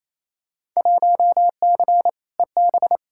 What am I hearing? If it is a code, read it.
Morse code, heard as a keyed tone, about 28 wpm, 700 Hz. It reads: 1C EB
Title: Field Day Exchanges Encore - 28wpm